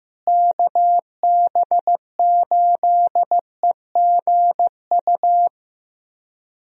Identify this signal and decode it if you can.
Morse code, heard as a keyed tone, about 15 wpm, 700 Hz. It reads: KB8EGU
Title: USA Call Signs - 6 Characters - Repeated 3x - 15wpm